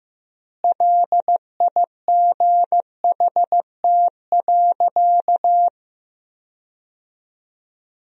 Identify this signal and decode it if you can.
Morse code, heard as a keyed tone, about 15 wpm, 700 Hz. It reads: LIGHT.
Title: 3x Word-Spacing - Sentences from Top 200 Words - 15wpm